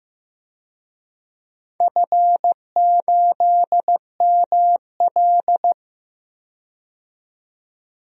Morse code, heard as a keyed tone, about 15 wpm, 700 Hz. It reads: F8ML
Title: DXCC-100 Call Signs - 4 Characters - 15wpm